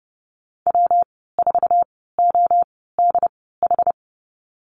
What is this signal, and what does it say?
Morse code, heard as a keyed tone, about 30 wpm, spaced as for 17 wpm, 700 Hz. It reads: W4ODH
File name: Lesson 19g - Letter w - Call Signs 30wpm